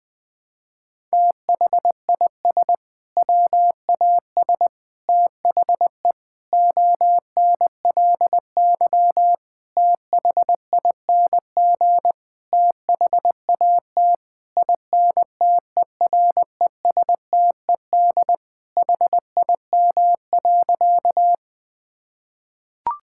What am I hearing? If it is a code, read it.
Morse code, heard as a keyed tone, about 20 wpm, 700 Hz. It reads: THIS WAS THE ONLY THING THAT INTERESTED HIM.
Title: Sentences from Top 1000 Words - Review Part 6 - 20wpm